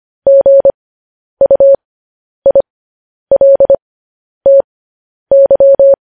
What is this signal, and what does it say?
Morse code, heard as a keyed tone, about 25 wpm, spaced as for 10 wpm, 550 Hz. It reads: GUILTY
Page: news-headlines-in-morse-code-at-10-wpm